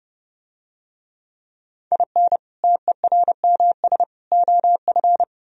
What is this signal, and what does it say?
Morse code, heard as a keyed tone, about 30 wpm, 700 Hz. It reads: IN TERMS OF